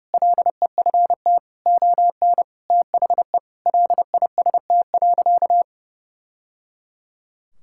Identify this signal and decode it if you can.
Morse code, heard as a keyed tone, about 30 wpm, 700 Hz. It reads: LEFT ON THE LIST.
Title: Sentences from Top 1000 Words - Review Part 6 - 30wpm